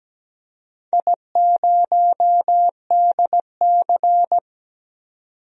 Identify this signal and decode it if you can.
Morse code, heard as a keyed tone, about 17 wpm, 700 Hz. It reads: I0DC
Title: DXCC-100 Call Signs - 4 Characters - 17wpm